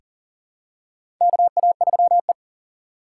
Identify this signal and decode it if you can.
Morse code, heard as a keyed tone, about 40 wpm, 700 Hz. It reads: KA3E
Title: USA Call Signs - 4 Characters - 2x Character Spacing - 40wpm